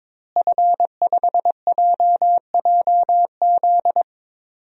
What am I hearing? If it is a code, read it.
Morse code, heard as a keyed tone, about 22 wpm, 700 Hz. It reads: F5JJZ
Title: DXCC-100 Call Signs - 5 Characters - Repeated 3x - 22wpm